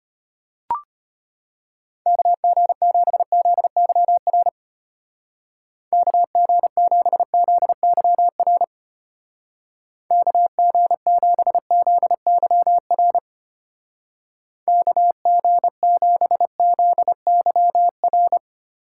Morse code, heard as a keyed tone, about 31 wpm, 700 Hz. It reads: KG7ZYR KG7ZYR KG7ZYR KG7ZYR
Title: Speed Racing - USA 2x3 Call Signs 25wpm